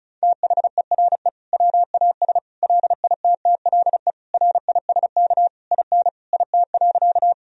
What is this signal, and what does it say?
Morse code, heard as a keyed tone, about 36 wpm, 700 Hz. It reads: THERE WAS LITTLE RISK IN IT.
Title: Sentences from Top 1000 Words - Review Part 4 - 35wpm